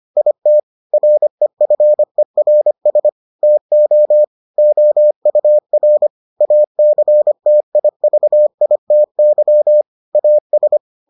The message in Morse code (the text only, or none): IT REFERS TO OUR ACTIVITY AS